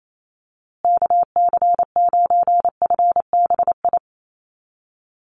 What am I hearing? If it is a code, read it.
Morse code, heard as a keyed tone, about 28 wpm, 700 Hz. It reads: KC9FBI
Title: USA Call Signs Repeated 2x 28wpm